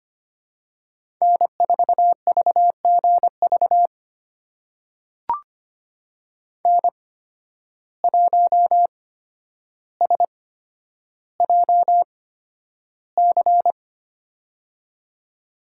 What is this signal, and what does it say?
Morse code, heard as a keyed tone, about 25 wpm, spaced as for 11 wpm, 700 Hz. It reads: N4VGV N1SJC
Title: USA Call Signs - 5 Characters - 8x Character Spacing - 25wpm